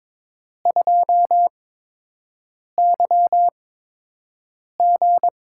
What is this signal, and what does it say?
Morse code, heard as a keyed tone, about 22 wpm, spaced as for 6 wpm, 700 Hz. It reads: 2YG
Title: USA Call Signs - 4 Characters - 8x Character Spacing - 22wpm